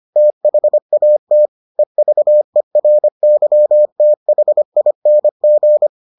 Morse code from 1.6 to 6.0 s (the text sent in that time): EVERYTHING